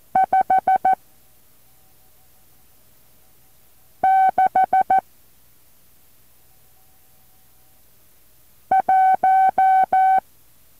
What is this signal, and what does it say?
Morse code, heard as a keyed tone, about 14 wpm, 760 Hz. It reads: 5 6 1
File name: intl morse code2